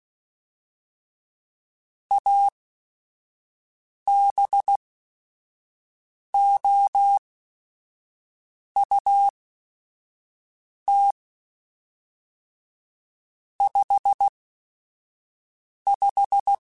Morse code, heard as a keyed tone, about 16 wpm, spaced as for 5 wpm, 800 Hz. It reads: ABOUT 55
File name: Weather Sets / Weather Sets-mp3s / Weather 4 24 @5